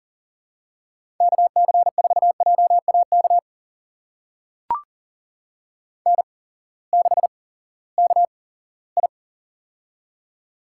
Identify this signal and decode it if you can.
Morse code, heard as a keyed tone, about 40 wpm, spaced as for 20 wpm, 700 Hz. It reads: KC4JAK N6KI
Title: USA Call Signs - 8x Character Spacing - 40wpm